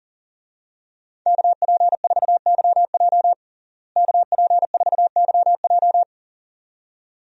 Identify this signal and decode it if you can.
Morse code, heard as a keyed tone, about 40 wpm, 700 Hz. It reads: KP4YJ KP4YJ
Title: DXCC-100 Call Signs - 5 Characters - Repeated 2x - 40wpm